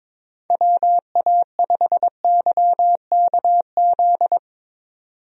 Morse code, heard as a keyed tone, about 21 wpm, 700 Hz. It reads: WA5YKZ
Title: USA Call Signs - 6 Characters - 22wpm